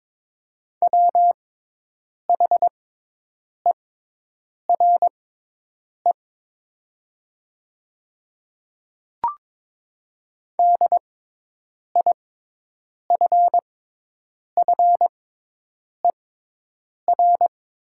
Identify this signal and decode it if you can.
Morse code, heard as a keyed tone, about 22 wpm, spaced as for 8 wpm, 700 Hz. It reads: WHERE DIFFER
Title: 6x Character Spacing - Top 500 Words - 22wpm